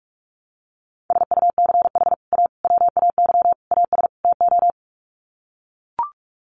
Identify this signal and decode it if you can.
Morse code, heard as a keyed tone, about 44 wpm, 700 Hz. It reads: SUCH A WAY AS TO
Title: Sets of 5 Words Encore2 45wpm